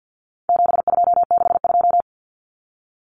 Morse code, heard as a keyed tone, about 50 wpm, 700 Hz. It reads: 7262
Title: Common Cut Numbers - 4-Digit 50wpm